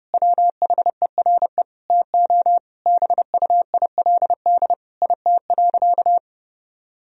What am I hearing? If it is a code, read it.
Morse code, heard as a keyed tone, about 32 wpm, 700 Hz. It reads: WHERE TO BUILD IT.
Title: Sentences from Top 1000 Words - Review Part 1 - 30wpm